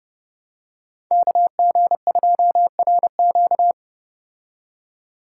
Morse code, heard as a keyed tone, about 30 wpm, 700 Hz. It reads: KG2RQ